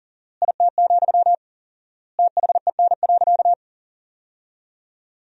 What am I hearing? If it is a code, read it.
Morse code, heard as a keyed tone, about 37 wpm, 700 Hz. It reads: IT, THEN.
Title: 4x Word-Spacing - Sentences from Top 200 Words - 40wpm